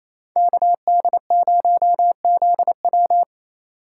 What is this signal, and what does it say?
Morse code, heard as a keyed tone, about 35 wpm, 700 Hz. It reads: KD0ZW